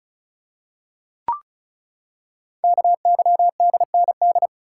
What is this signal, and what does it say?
Morse code, heard as a keyed tone, about 36 wpm, 700 Hz. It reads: KYDND
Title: Code Groups - 5 Letters - 35wpm